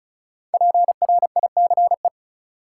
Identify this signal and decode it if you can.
Morse code, heard as a keyed tone, about 36 wpm, 700 Hz. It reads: PRICE
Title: Sentences from Top 600 Words - New Word Drill - 35wpm